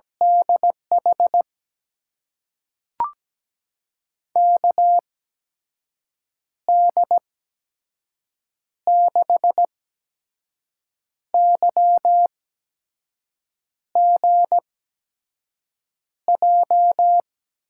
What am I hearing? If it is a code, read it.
Morse code, heard as a keyed tone, about 17 wpm, spaced as for 5 wpm, 700 Hz. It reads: DH KD6YGJ
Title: USA Call Signs - 6 Characters - 8x Character Spacing - 17wpm